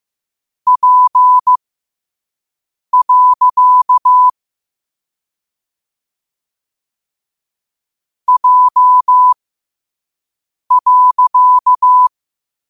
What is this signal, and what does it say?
Morse code, heard as a keyed tone, about 15 wpm, spaced as for 5 wpm, 1000 Hz. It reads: P. J.
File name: qotdfiles / 5wpm